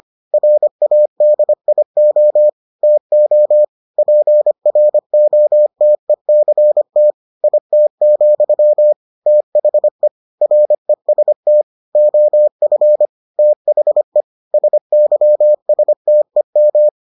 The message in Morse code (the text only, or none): RADIO TO PROTECT IT, THE REST OF THE SYSTEM